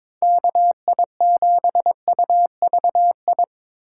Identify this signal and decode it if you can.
Morse code, heard as a keyed tone, about 26 wpm, 700 Hz. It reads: KI7UVI
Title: Speed Racing - USA Call Signs 22wpm